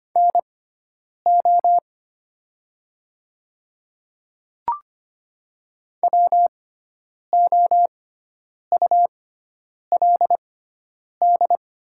Morse code, heard as a keyed tone, about 25 wpm, spaced as for 9 wpm, 700 Hz. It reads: NO WOULD
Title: 6x Character Spacing - Top 300 Words - 25wpm